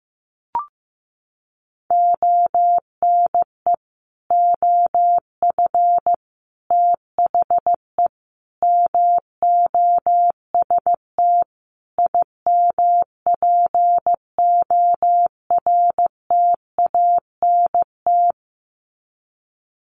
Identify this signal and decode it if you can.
Morse code, heard as a keyed tone, about 15 wpm, 700 Hz. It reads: ONE OF THE MOST IMPORTANT
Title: Sets of 5 Words 15wpm